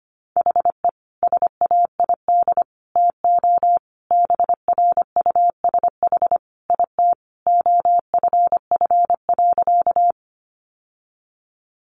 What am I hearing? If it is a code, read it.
Morse code, heard as a keyed tone, about 25 wpm, 700 Hz. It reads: HE SAID TO BRUSH IT OFF.
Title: Sentences from Top 1100 Words 25wpm